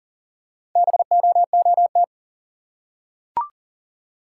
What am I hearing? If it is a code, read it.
Morse code, heard as a keyed tone, about 40 wpm, 700 Hz. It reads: BOOT
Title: Four Letter Words - 40wpm